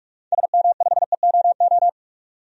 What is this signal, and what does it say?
Morse code, heard as a keyed tone, about 44 wpm, 700 Hz. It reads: SM5EOO
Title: DXCC-100 Call Signs - 6 Characters - 45wpm